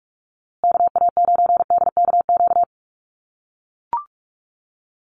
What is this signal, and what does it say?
Morse code, heard as a keyed tone, about 44 wpm, 700 Hz. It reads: KA9DKQ
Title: USA Call Signs Repeated 3x 45wpm